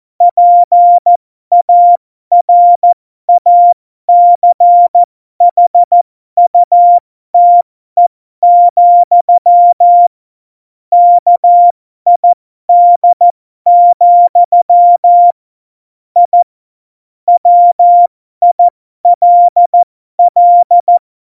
Morse code, heard as a keyed tone, about 14 wpm, spaced as for 12 wpm, 700 Hz. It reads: PARACHUTE, KID, I WILL